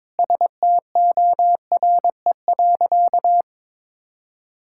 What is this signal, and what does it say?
Morse code, heard as a keyed tone, about 21 wpm, 700 Hz. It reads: STORE.
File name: practice / 22wpm / Sentences with 2-Word Phrases - 22wpm